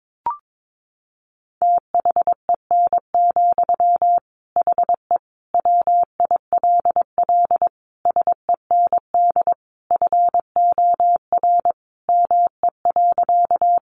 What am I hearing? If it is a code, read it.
Morse code, heard as a keyed tone, about 22 wpm, 700 Hz. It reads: THEN, HE WILL SEND FOR ME.
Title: Sentences from Top 1500 Words 22wpm